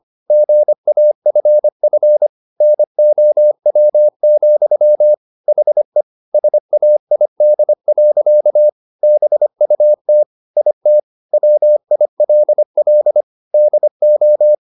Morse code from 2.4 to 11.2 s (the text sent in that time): NOW, HE SAID. BUT IT